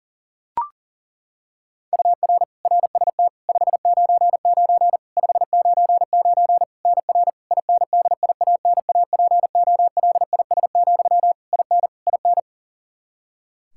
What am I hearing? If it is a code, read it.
Morse code, heard as a keyed tone, about 40 wpm, 700 Hz. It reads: UR RST 599 599 NR INDIANAPOLIS, IN IN
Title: Practice QSO - Exchange 1 - Beginning - 40wpm